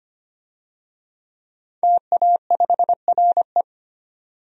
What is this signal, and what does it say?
Morse code, heard as a keyed tone, about 25 wpm, 700 Hz. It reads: TA5RE
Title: DXCC-100 Call Signs - 5 Characters - 25wpm